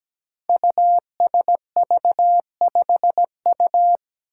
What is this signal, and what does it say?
Morse code, heard as a keyed tone, about 17 wpm, 700 Hz. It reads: USV5U
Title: Five Challenging Letter-Number Combinations 17wpm